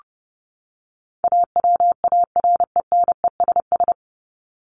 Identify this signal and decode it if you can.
Morse code, heard as a keyed tone, about 29 wpm, 700 Hz. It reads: AWARENESS
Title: Top 1501-1750 Words 30wpm